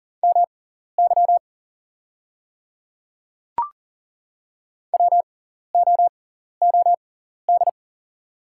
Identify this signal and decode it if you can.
Morse code, heard as a keyed tone, about 40 wpm, spaced as for 14 wpm, 700 Hz. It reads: MY WOOD
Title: 6x Character Spacing - Top 300 Words - 40wpm